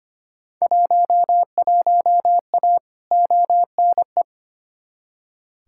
Morse code, heard as a keyed tone, about 25 wpm, 700 Hz. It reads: 11A ONE
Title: Field Day Exchanges Encore - 25wpm